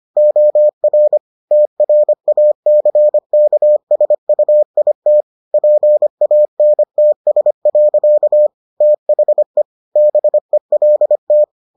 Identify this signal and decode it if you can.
Morse code, heard as a keyed tone, about 25 wpm, 600 Hz. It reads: OR TRACKSUIT PANTS. THE BELT